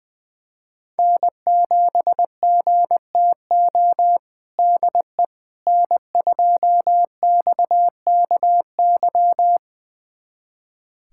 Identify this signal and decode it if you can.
Morse code, heard as a keyed tone, about 20 wpm, 700 Hz. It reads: N7GTO DE N2XKY
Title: USA Callsign DE Callsign - 5 Characters - 20wpm